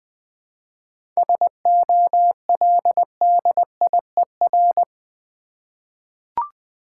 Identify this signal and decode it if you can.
Morse code, heard as a keyed tone, about 20 wpm, 700 Hz. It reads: SOLDIER
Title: 7 Letter Words - First 3 Letters Given - 20wpm